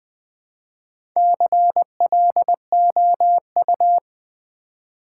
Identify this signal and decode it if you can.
Morse code, heard as a keyed tone, about 20 wpm, 700 Hz. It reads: CLOU
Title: Build A Word - 1 Letter At A Time - 5-Letter Words - 20wpm